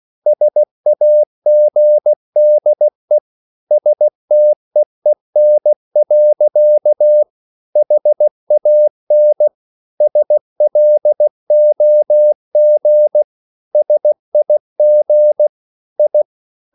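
Morse code, heard as a keyed tone, about 16 wpm, 600 Hz. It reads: SAGDE STEEN. HAN SLOG SIG I
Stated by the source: Slow (QRS) Morse code at 16 wpm, from CW med Gnister.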